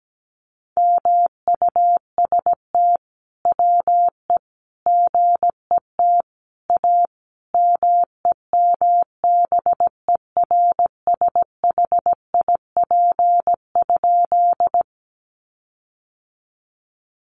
Morse code, heard as a keyed tone, about 17 wpm, 700 Hz. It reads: MUST WE GET A MEMBERSHIP?
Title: Sentences from Top 1400 Words 17wpm